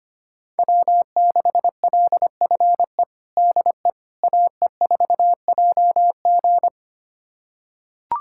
Morse code, heard as a keyed tone, about 25 wpm, 700 Hz. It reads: W6LFE DE AE4JG
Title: USA Callsign DE Callsign - 5 Characters - 25wpm